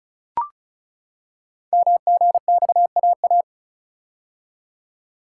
Morse code, heard as a keyed tone, about 34 wpm, 700 Hz. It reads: MGXAA
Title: Code Groups - 5 Letters - 35wpm